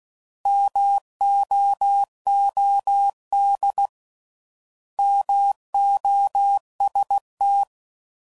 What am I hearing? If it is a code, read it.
Morse code, heard as a keyed tone, about 16 wpm, 800 Hz. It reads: MOOD MOST